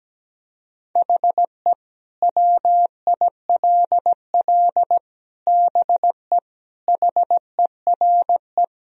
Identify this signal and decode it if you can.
Morse code, heard as a keyed tone, about 17 wpm, 700 Hz. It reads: HE WILL BE HERE
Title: Sentences from Top 1000 Words - Review Part 3 - 17wpm